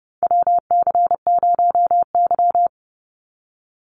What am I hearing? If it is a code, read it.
Morse code, heard as a keyed tone, about 30 wpm, 700 Hz. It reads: WC0Y
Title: NA SOTA Chaser Callsigns 30wpm